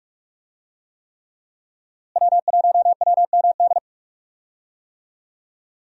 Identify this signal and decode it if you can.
Morse code, heard as a keyed tone, about 44 wpm, 700 Hz. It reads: W1WMD
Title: USA Call Signs - 5 Characters - Repeated 3x - 45wpm